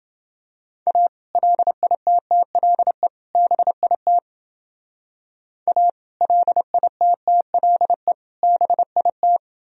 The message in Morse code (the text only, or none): A LITTLE BIT A LITTLE BIT